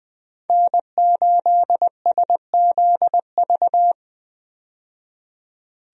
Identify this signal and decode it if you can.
Morse code, heard as a keyed tone, about 20 wpm, 700 Hz. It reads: N8SZV
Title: USA Call Signs - 5 Characters - 20wpm